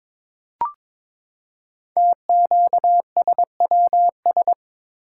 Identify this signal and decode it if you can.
Morse code, heard as a keyed tone, about 22 wpm, 700 Hz. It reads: TQSWS
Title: Code Groups - 5 Letters - 22wpm